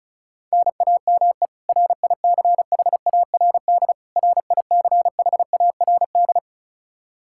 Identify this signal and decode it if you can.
Morse code, heard as a keyed tone, about 36 wpm, 700 Hz. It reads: NAME RICHARD RICHARD
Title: Practice QSO - Exchange 1 - Beginning - 35wpm